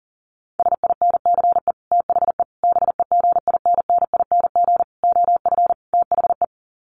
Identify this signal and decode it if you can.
Morse code, heard as a keyed tone, about 40 wpm, 700 Hz. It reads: SINCE THE BEGINNING OF THE
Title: Sets of 5 Words Encore2 40wpm